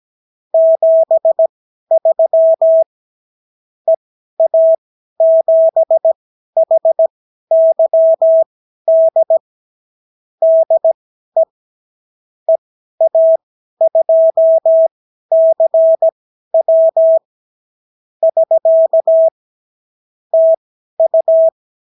Morse code, heard as a keyed tone, about 17 wpm, spaced as for 12 wpm, 650 Hz. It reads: 73 EA7HYD DE EA2CW <SK> TU
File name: CW QSO libroCW4 18-8ppm